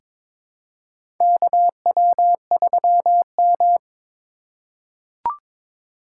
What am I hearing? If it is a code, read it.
Morse code, heard as a keyed tone, about 22 wpm, 700 Hz. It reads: KW3M
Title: USA Call Signs - 4 Characters - 3x Character Spacing - 22wpm